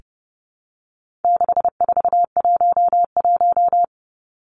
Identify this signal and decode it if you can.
Morse code, heard as a keyed tone, about 30 wpm, 700 Hz. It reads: 6411
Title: Numbers - 4-Digit 30wpm